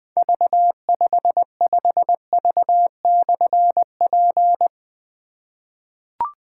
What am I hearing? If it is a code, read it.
Morse code, heard as a keyed tone, about 20 wpm, 700 Hz. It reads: V55V/P
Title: Difficult Call Signs 20wpm